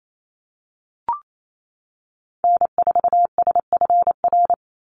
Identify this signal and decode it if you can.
Morse code, heard as a keyed tone, about 28 wpm, 700 Hz. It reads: N4SFR
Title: NA SOTA Activator Callsigns 28wpm